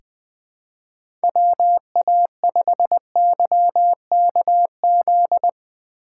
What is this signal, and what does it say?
Morse code, heard as a keyed tone, about 20 wpm, 700 Hz. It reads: WA5YKZ
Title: USA Call Signs - 6x Character Spacing - 20wpm